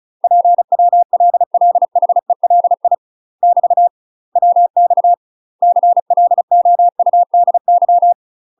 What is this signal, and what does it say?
Morse code, heard as a keyed tone, about 35 wpm, 700 Hz. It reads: PWLLHELI <BT> WX CLOUDY
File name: UK MASTER QSOs TRAINING / 35WPM / UK MASTER QSOs 0001